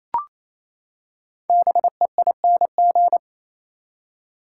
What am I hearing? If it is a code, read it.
Morse code, heard as a keyed tone, about 28 wpm, 700 Hz. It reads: BEING
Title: Five Letter Words - 28wpm